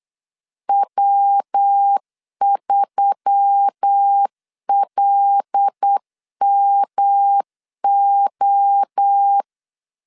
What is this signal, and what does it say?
Morse code, heard as a keyed tone, about 8 wpm, 800 Hz. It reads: W3LMO
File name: LEZIONI / 8 YJ45 / 8 YJ 8wpm 3313